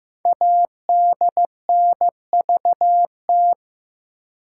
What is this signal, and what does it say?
Morse code, heard as a keyed tone, about 15 wpm, 700 Hz. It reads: ADNVT
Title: Code Groups - 5 Letters - 15wpm